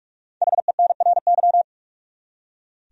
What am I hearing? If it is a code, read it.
Morse code, heard as a keyed tone, about 46 wpm, 700 Hz. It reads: HENRY
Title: USA Common Names - 45wpm